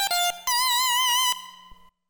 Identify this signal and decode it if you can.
There is no code.